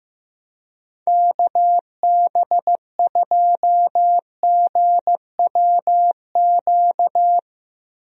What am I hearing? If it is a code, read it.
Morse code, heard as a keyed tone, about 15 wpm, 700 Hz. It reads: KB2GWQ
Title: USA Call Signs - 6 Characters - Repeated 3x - 15wpm